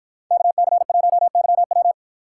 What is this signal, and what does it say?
Morse code, heard as a keyed tone, about 52 wpm, 700 Hz. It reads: KC1YW